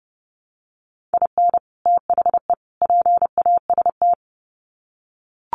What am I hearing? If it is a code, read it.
Morse code, heard as a keyed tone, about 30 wpm, 700 Hz. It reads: IN THE PAST
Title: Sets of 3 Words 30wpm